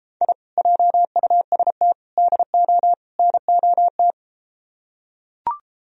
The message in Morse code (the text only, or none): I JUST DO NOT